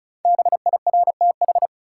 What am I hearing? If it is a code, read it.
Morse code, heard as a keyed tone, about 34 wpm, 700 Hz. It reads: BIRTH
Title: Sentences from Top 800 Words - New Word Drill - 35wpm